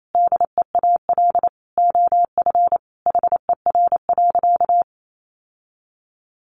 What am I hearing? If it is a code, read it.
Morse code, heard as a keyed tone, about 28 wpm, 700 Hz. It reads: DEAL OF HER.
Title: Sentences from Top 800 Words 28wpm